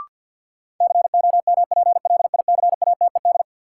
Morse code, heard as a keyed tone, about 51 wpm, 700 Hz. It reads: COMPLICATED
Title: Sentences from Top 1200 Words - New Word Drill Alt - 50wpm